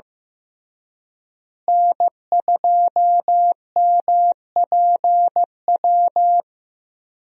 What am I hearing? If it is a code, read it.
Morse code, heard as a keyed tone, about 15 wpm, 700 Hz. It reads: N2MPW
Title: USA Call Signs - 5 Characters - Repeated 3x - 15wpm